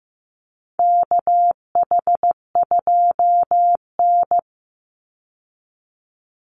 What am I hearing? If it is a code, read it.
Morse code, heard as a keyed tone, about 15 wpm, 700 Hz. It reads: KH2N
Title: Call Signs Repeated 3x - 15wpm